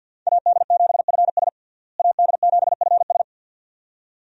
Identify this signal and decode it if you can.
Morse code, heard as a keyed tone, about 47 wpm, 700 Hz. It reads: AD7FS AD7FS
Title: USA Call Signs - 5 Characters - Repeated 2x - 50wpm